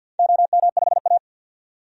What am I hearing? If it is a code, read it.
Morse code, heard as a keyed tone, about 50 wpm, 700 Hz. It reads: OM5A
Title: DXCC-100 Call Signs - 4 Characters - 50wpm